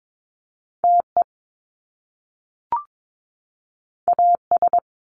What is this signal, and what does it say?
Morse code, heard as a keyed tone, about 22 wpm, 700 Hz. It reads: TE AS
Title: Top Two Letter Combinations in Words 22wpm